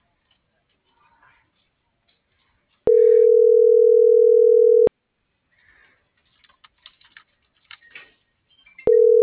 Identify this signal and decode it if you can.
Ringback tone (North America), 440 + 480 Hz.
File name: CENTRAL SAGOT / out xquenda.sagot - .76590